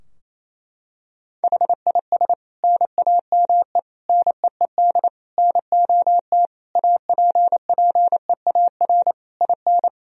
Morse code, heard as a keyed tone, about 27 wpm, 700 Hz. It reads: HIS NAME NEED NOT APPEAR IN
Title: Sentences from Top 1000 Words - Review Part 2 - 28wpm